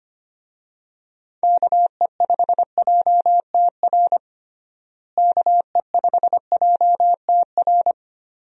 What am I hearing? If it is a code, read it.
Morse code, heard as a keyed tone, about 25 wpm, 700 Hz. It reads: KE5JTR KE5JTR
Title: USA Call Signs - 6 Characters - Repeated 3x - 25wpm